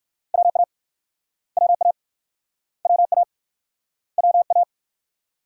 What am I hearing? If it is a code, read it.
Morse code, heard as a keyed tone, about 56 wpm, 700 Hz. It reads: WA WA WA WA
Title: Speed-Racing Multi-letter Phonemes - 45wpm